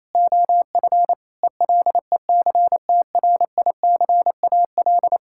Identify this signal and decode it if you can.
Morse code, heard as a keyed tone, about 28 wpm, 700 Hz. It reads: OF ELECTRICAL
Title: Sentences from Top 1000 Words - Review Part 6 - 28wpm